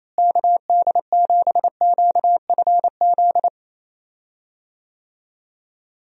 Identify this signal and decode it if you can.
Morse code, heard as a keyed tone, about 28 wpm, 700 Hz. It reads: KD7QFZ
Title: USA Call Signs - 6 Characters - Repeated 3x - 28wpm